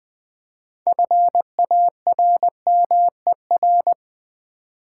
Morse code, heard as a keyed tone, about 20 wpm, 700 Hz. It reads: FARMER
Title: 6 Letter Words - First 3 Letters Given - 20wpm